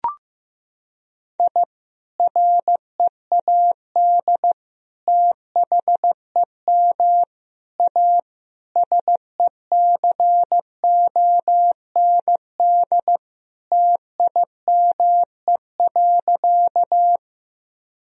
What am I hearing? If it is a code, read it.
Morse code, heard as a keyed tone, about 15 wpm, 700 Hz. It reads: I READ THEM A SECOND TIME.
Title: Sentences from Top 1000 Words - Review Part 1 - 15wpm